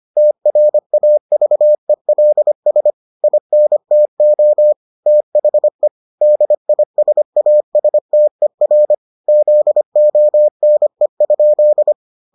Morse code, heard as a keyed tone, about 25 wpm, 600 Hz. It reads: TRAVELS INTO THE DISASTER ZONE?